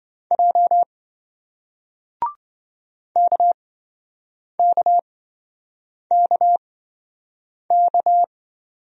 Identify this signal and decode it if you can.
Morse code, heard as a keyed tone, about 27 wpm, spaced as for 7 wpm, 700 Hz. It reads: J KKKK